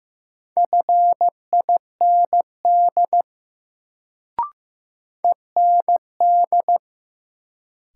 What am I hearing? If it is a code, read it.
Morse code, heard as a keyed tone, about 15 wpm, 700 Hz. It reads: FIND END
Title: Sets of 2 Words - New Word Drill - 15wpm